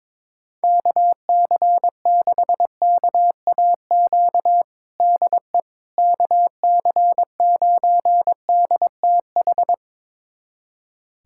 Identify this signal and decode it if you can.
Morse code, heard as a keyed tone, about 22 wpm, 700 Hz. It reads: KC6KAQ DE KC9DTH
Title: USA Callsign DE Callsign - 6 Characters - 22wpm